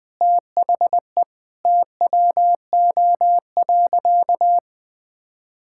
Morse code, heard as a keyed tone, about 20 wpm, 700 Hz. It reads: THE TWO.
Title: Sentences with 2-Word Phrases - Spoken Beginning - 20wpm